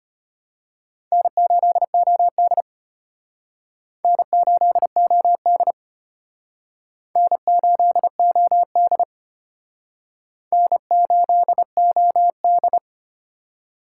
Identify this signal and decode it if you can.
Morse code, heard as a keyed tone, about 31 wpm, 700 Hz. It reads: N8OD N8OD N8OD N8OD
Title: Speed Racing - USA 1x2 Call Signs 25wpm